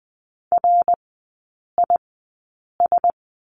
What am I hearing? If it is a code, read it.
Morse code, heard as a keyed tone, about 20 wpm, spaced as for 8 wpm, 700 Hz. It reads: RIS
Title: 053 Lesson 10e - Letter r - Three Characters 20wpm